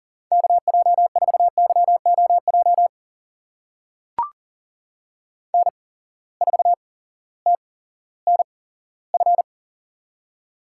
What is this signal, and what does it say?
Morse code, heard as a keyed tone, about 40 wpm, spaced as for 18 wpm, 700 Hz. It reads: KJ4YOJ N4TNF
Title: USA Call Signs - 8x Character Spacing - 40wpm